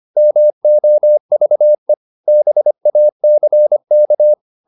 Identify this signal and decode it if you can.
Morse code, heard as a keyed tone, about 25 wpm, 600 Hz. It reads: MOVE BACK